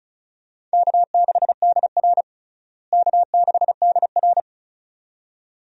Morse code, heard as a keyed tone, about 34 wpm, 700 Hz. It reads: K6DR K6DR
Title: USA Call Signs - 4 Characters - Repeated 2x - 35wpm